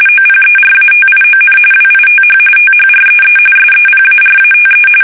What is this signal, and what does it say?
RTTY, about 45.45 baud. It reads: OPERATION WITH THREE SEPARATE
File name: fsk850 / USS-Massachusetts